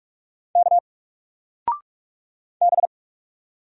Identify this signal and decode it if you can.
Morse code, heard as a keyed tone, about 46 wpm, 700 Hz. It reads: K B
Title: Single Character (Letter:Number:Punctuation:Prosign) 45wpm